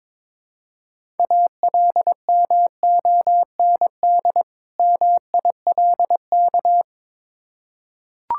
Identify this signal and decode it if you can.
Morse code, heard as a keyed tone, about 22 wpm, 700 Hz. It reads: ALMOND MILK
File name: Difficult Words with Hints - 22wpm